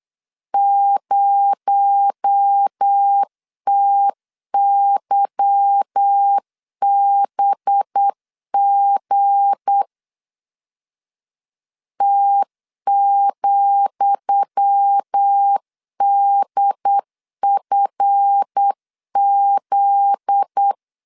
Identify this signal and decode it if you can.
Morse code, heard as a keyed tone, about 8 wpm, 800 Hz. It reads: 0TYBG T,DFZ
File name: LEZIONI / 10 SIGNS / 10D COMA 8wpm 3749